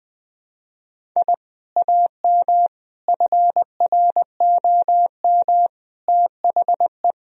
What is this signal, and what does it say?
Morse code, heard as a keyed tone, about 20 wpm, 700 Hz. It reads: I AM FROM THE
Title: Sentences from Top 1000 Words - Review Part 1 - 20wpm